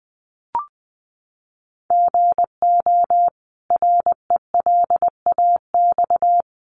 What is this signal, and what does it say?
Morse code, heard as a keyed tone, about 20 wpm, 700 Hz. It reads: GO RELAX
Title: Sentences from Top 1400 Words 20wpm